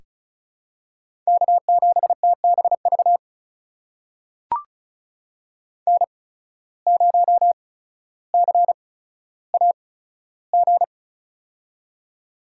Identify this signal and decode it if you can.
Morse code, heard as a keyed tone, about 36 wpm, spaced as for 15 wpm, 700 Hz. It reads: K7TBV N0CAG
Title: USA Call Signs - 5 Characters - 8x Character Spacing - 35wpm